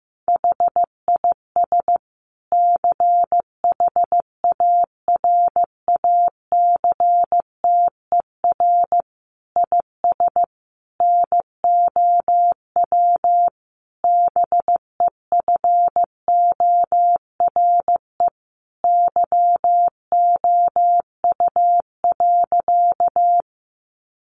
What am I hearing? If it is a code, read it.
Morse code, heard as a keyed tone, about 15 wpm, 700 Hz. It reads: HIS CHARACTER IS NOW BEFORE YOU.
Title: Sentences from Top 1200 Words 15wpm